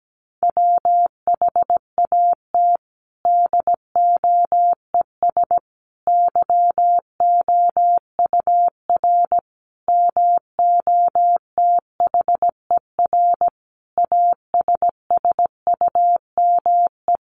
Morse code, heard as a keyed tone, about 17 wpm, 700 Hz. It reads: WHAT DOES YOUR MOTHER ASSUME
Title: Sentences from Top 1500 Words 17wpm